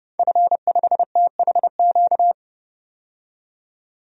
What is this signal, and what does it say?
Morse code, heard as a keyed tone, about 30 wpm, 700 Hz. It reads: F5THQ
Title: DXCC-100 Call Signs - 5 Characters - 30wpm